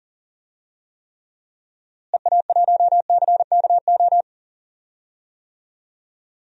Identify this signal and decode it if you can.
Morse code, heard as a keyed tone, about 40 wpm, 700 Hz. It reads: EA1CKO
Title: DXCC-100 Call Signs - 6 Characters - Repeated 2x - 40wpm